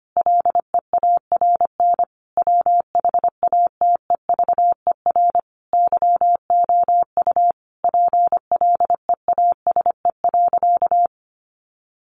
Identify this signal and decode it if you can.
Morse code, heard as a keyed tone, about 25 wpm, 700 Hz. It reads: LEARN WHATEVER YOU PLEASE.
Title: Sentences from Top 1500 Words 25wpm